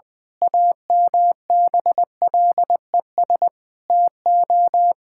Morse code, heard as a keyed tone, about 20 wpm, 700 Hz. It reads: AMBLES TO
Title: Pangram Challenge v2 - 20wpm